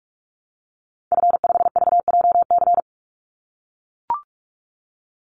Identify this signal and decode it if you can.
Morse code, heard as a keyed tone, about 44 wpm, 700 Hz. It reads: F5VJC
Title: Call Signs - 45wpm